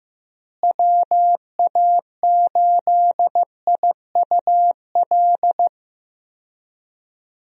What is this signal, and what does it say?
Morse code, heard as a keyed tone, about 15 wpm, 700 Hz. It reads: WA8IUL
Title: USA Call Signs - 6 Characters - Repeated 3x - 15wpm